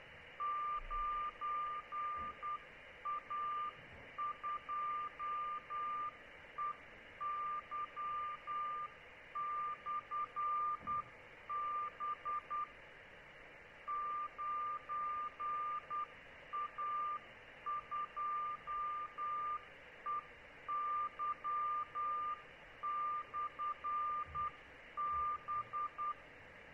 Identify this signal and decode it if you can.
Morse code, heard as a keyed tone, about 10 wpm, 1190 Hz, with noise. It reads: 9A2EY/B 9A2EY/B